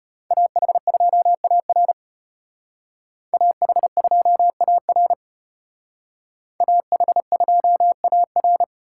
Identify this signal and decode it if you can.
Morse code, heard as a keyed tone, about 32 wpm, 700 Hz. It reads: AH2AR AH2AR AH2AR